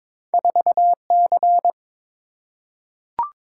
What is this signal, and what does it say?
Morse code, heard as a keyed tone, about 22 wpm, 700 Hz. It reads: 4C
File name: practice / 22wpm / Two Letter-Number Combinations - 22wpm